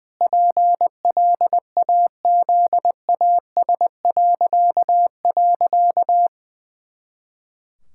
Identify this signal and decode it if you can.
Morse code, heard as a keyed tone, about 20 wpm, 700 Hz. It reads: PLAZAS..
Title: Pangram Challenge v2 - 20wpm